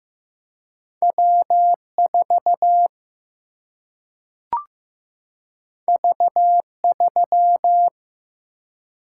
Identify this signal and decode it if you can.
Morse code, heard as a keyed tone, about 15 wpm, 700 Hz. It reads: W4 V3
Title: Single-Letter Single-Number 15wpm